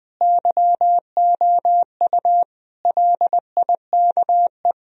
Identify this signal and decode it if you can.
Morse code, heard as a keyed tone, about 21 wpm, 700 Hz. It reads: YOU LIKE
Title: Sentences from Top 1000 Words - Review Part 1 - 20wpm